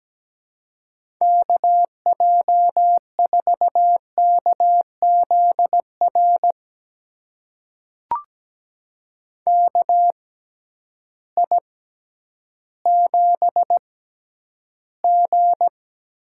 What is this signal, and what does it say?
Morse code, heard as a keyed tone, about 17 wpm, spaced as for 10 wpm, 700 Hz. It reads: KJ4KZR KI7G
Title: USA Call Signs - 6 Characters - 6x Character Spacing - 17wpm